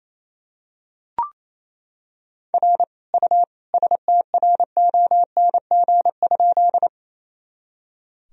Morse code, heard as a keyed tone, about 28 wpm, 700 Hz. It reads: R U STRONG?
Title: Sentences from Top 500 Words and Abbreviations 28wpm